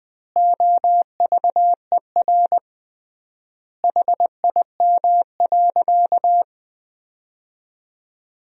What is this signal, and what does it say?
Morse code, heard as a keyed tone, about 19 wpm, 700 Hz. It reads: OVER HIM.
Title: 3x Word-Spacing - Sentences from Top 300 Words - 20wpm